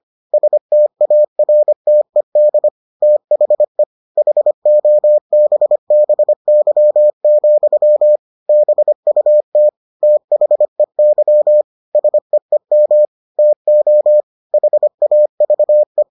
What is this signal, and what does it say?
Morse code, heard as a keyed tone, about 25 wpm, 600 Hz. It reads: STARTED THE HOBBY, BUT THEY SEEM TO HAVE